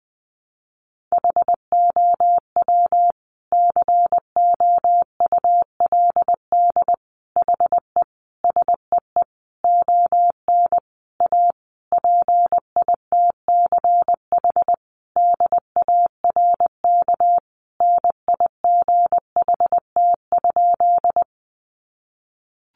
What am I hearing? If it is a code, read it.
Morse code, heard as a keyed tone, about 20 wpm, 700 Hz. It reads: HOW COULD HE SEE ON A PITCH DARK NIGHT?
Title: Sentences from Top 1300 Words 20wpm